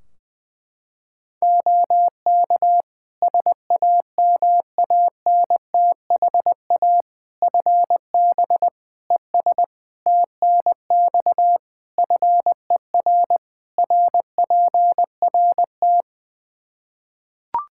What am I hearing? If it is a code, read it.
Morse code, heard as a keyed tone, about 20 wpm, 700 Hz. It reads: OK SAMANTHA FB ES TNX FER RPRT
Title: Practice QSO - Exchange 2 - Info - 20wpm